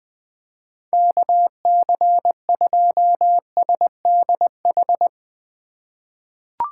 Morse code, heard as a keyed tone, about 21 wpm, 700 Hz. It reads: KC2SDH
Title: USA Call Signs - 2x Character Spacing - 20wpm